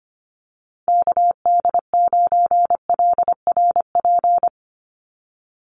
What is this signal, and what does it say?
Morse code, heard as a keyed tone, about 25 wpm, 690 Hz. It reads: KD9LRP
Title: USA Call Signs 25wpm